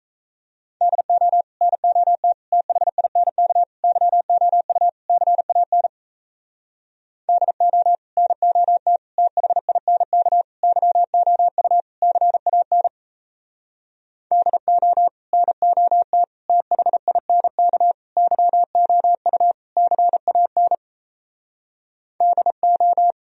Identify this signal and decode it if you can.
Morse code, heard as a keyed tone, about 37 wpm, 700 Hz. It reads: DO NOT THINK YOU CAN DO NOT THINK YOU CAN DO NOT THINK YOU CAN DO